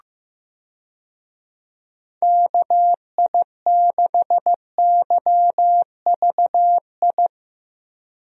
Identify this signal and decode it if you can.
Morse code, heard as a keyed tone, about 15 wpm, 700 Hz. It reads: KI6YVI
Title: USA Call Signs - 6 Characters - Repeated 2x - 15wpm